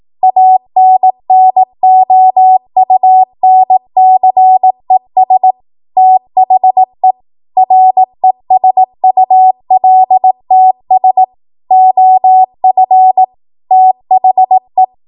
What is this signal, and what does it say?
Morse code, heard as a keyed tone, about 18 wpm, 750 Hz. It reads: ANNOUNCES THE RESULTS OF THE